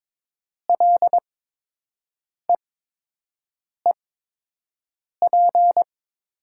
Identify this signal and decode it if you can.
Morse code, heard as a keyed tone, about 22 wpm, spaced as for 6 wpm, 700 Hz. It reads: LEEP
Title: 8x Character Spacing - Top 500 Words - 22wpm